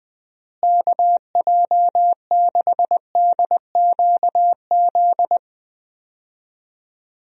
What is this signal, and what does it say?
Morse code, heard as a keyed tone, about 20 wpm, 700 Hz. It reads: KJ6DQZ
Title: USA Call Signs - 6 Characters - Repeated 2x - 20wpm